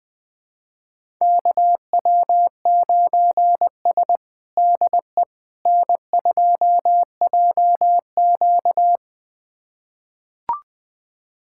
Morse code, heard as a keyed tone, about 20 wpm, 700 Hz. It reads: KW9S DE N2JQ
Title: USA Callsign DE Callsign - 4 Characters - 20wpm